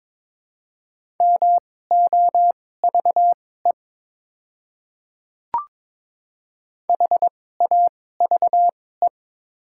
Morse code, heard as a keyed tone, about 22 wpm, spaced as for 16 wpm, 700 Hz. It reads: MOVE HAVE
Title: 2x Character Spacing - Top 300 Words - 22wpm